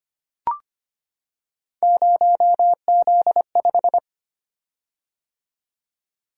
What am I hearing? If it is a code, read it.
Morse code, heard as a keyed tone, about 25 wpm, 700 Hz. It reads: 0Z5
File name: Three Letter-Number Combinations - 25wpm